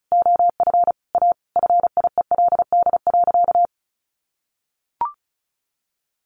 Morse code, heard as a keyed tone, about 34 wpm, 700 Hz. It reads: OF A FIELD.
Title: Sentences from Top 1000 Words 35wpm